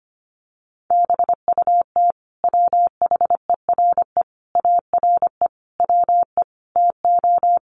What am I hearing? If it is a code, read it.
Morse code, heard as a keyed tone, about 25 wpm, 700 Hz. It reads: BUT WHERE ARE WE TO
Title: Sentences from Top 200 Words 25wpm